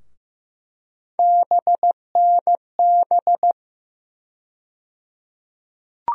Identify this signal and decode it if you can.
Morse code, heard as a keyed tone, about 15 wpm, 700 Hz. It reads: BNB
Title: USA Call Sign Postfixes - 3 Letters 15wpm